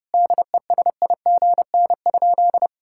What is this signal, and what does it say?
Morse code, heard as a keyed tone, about 30 wpm, 700 Hz. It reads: DESIGN?
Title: Sentences from Top 1000 Words - Review Part 5 - 30wpm